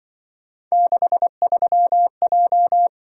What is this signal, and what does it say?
Morse code, heard as a keyed tone, about 25 wpm, 700 Hz. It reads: 63J
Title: Speed-Racing - Three Letter-Number Combinations - 20wpm